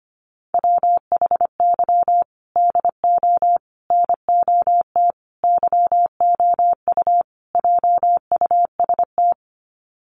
Morse code, heard as a keyed tone, about 25 wpm, 700 Hz. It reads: WHY DO NOT YOU JUST
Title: Sets of 5 Words 25wpm